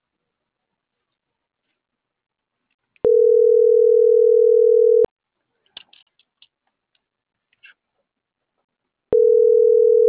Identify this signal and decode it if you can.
Ringback tone (North America), 440 + 480 Hz.